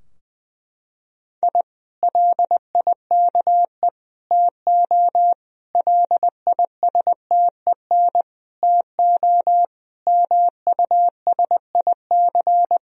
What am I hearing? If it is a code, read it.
Morse code, heard as a keyed tone, about 20 wpm, 700 Hz. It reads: I LIKE TO LISTEN TO MUSIC
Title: Sentences from Top 1000 Words - Review Part 3 - 20wpm